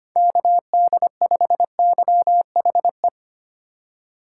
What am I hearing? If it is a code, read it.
Morse code, heard as a keyed tone, about 30 wpm, 700 Hz. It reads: KD5YHE
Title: Speed Racing - USA Call Signs 25wpm